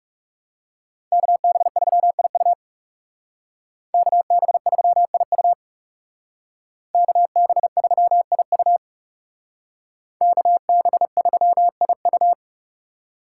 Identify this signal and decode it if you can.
Morse code, heard as a keyed tone, about 35 wpm, 700 Hz. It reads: KB3IU KB3IU KB3IU KB3IU
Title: Speed Racing - USA Call Signs 30wpm